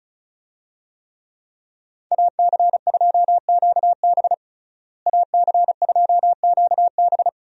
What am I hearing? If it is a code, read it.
Morse code, heard as a keyed tone, about 34 wpm, 700 Hz. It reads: AC2QB AC2QB
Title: USA Call Signs - 5 Characters - Repeated 2x - 35wpm